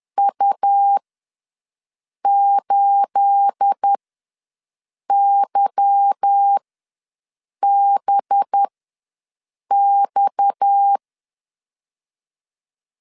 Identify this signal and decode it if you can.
Morse code, heard as a keyed tone, about 10 wpm, spaced as for 6 wpm, 800 Hz. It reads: U8YBX
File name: LEZIONI / 10 SIGNS / 10A SLASH ws 10wpm 4121